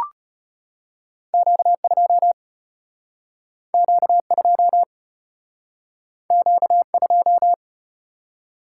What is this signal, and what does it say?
Morse code, heard as a keyed tone, about 34 wpm, 700 Hz. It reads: Q2 Q2 Q2